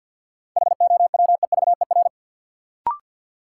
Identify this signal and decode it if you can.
Morse code, heard as a keyed tone, about 52 wpm, 700 Hz. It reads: HOWEVER
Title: Sentences from Top 600 Words - New Word Drill - 50wpm